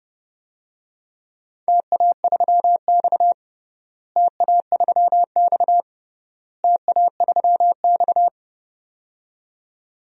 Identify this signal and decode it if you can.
Morse code, heard as a keyed tone, about 29 wpm, 700 Hz. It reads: TA3X TA3X TA3X
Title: DXCC-100 Call Signs - 4 Characters - Repeated 3x - 30wpm